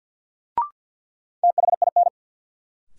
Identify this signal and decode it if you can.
Morse code, heard as a keyed tone, about 53 wpm, 700 Hz. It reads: THIN
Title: Sentences from Top 1200 Words - New Word Drill Alt - 50wpm